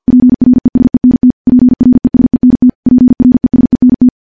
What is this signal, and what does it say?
Morse code, heard as a keyed tone, about 50 wpm, 250 Hz. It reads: PRESENT PRESENT PRESENT
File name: morse-training-50WPM